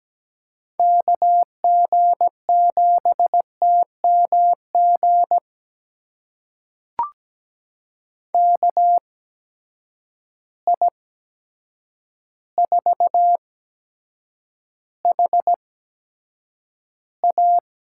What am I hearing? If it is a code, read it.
Morse code, heard as a keyed tone, about 17 wpm, spaced as for 8 wpm, 700 Hz. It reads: KG7TMG KI4HA
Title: USA Call Signs - 6 Characters - 8x Character Spacing - 17wpm